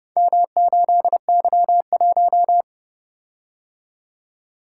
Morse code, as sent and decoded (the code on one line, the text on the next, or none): -- ---.. -.-- .----
M8Y1